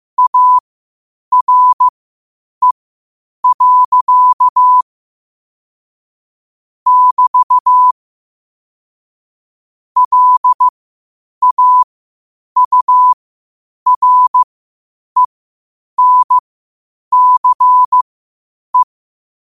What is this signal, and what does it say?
Morse code, heard as a keyed tone, about 15 wpm, spaced as for 8 wpm, 1000 Hz. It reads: ARE. <BT> LAURENCE
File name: qotdfiles / 8wpm